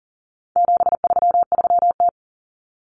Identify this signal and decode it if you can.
Morse code, heard as a keyed tone, about 40 wpm, 700 Hz. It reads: 733T
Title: Common Cut Numbers - 4-Digit 40wpm